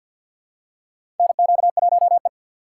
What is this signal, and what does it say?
Morse code, heard as a keyed tone, about 49 wpm, 700 Hz. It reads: NQ1E
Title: USA Call Signs - 4 Characters - 3x Character Spacing - 50wpm